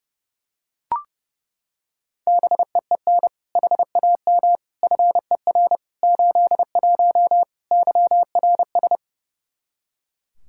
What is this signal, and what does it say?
Morse code, heard as a keyed tone, about 30 wpm, 700 Hz. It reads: BEEN HAM FER 81 YRS
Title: Practice QSO - Exchange 3 - More Info - 30wpm